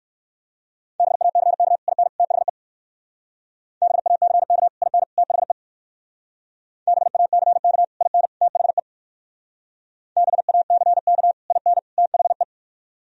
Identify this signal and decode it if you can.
Morse code, heard as a keyed tone, about 57 wpm, 700 Hz. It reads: BACK IN THE BACK IN THE BACK IN THE BACK IN THE